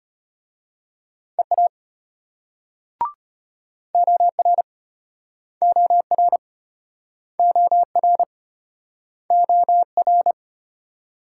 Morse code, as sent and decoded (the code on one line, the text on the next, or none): . .- / --- .-. / --- .-. / --- .-. / --- .-.
EA OR OR OR OR